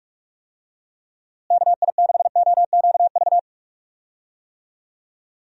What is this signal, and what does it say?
Morse code, heard as a keyed tone, about 46 wpm, 700 Hz. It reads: KI6OQV
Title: USA Call Signs - 6 Characters - Repeated 2x - 45wpm